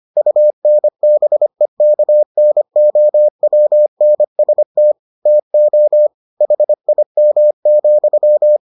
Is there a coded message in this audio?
Yes: Morse code from 0.2 to 8.6 s, heard as a keyed tone, about 25 wpm, 600 Hz. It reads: UNBEKNOWNST TO HIM,